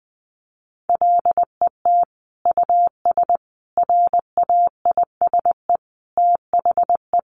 Morse code, heard as a keyed tone, about 20 wpm, 700 Hz. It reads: LET US RAISE THE
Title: Sentences from Top 1400 Words 20wpm